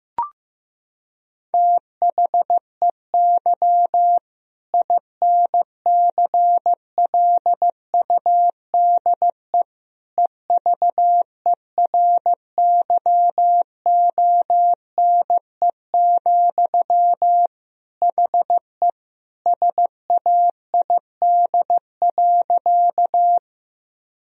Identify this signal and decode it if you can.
Morse code, heard as a keyed tone, about 15 wpm, 700 Hz. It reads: THEY INCLUDE EVERYONE, HE SAID.
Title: Sentences from Top 1000 Words - Review Part 3 - 15wpm